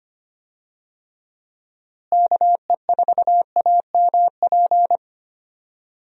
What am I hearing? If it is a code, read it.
Morse code, heard as a keyed tone, about 25 wpm, 700 Hz. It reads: KE4AMP
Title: USA Call Signs - 6 Characters - Repeated 2x - 25wpm